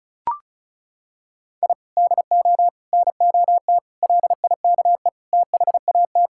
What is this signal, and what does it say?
Morse code, heard as a keyed tone, about 36 wpm, 700 Hz. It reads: I DO NOT LIKE THAT
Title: Sentences from Top 1000 Words - Review Part 6 - 35wpm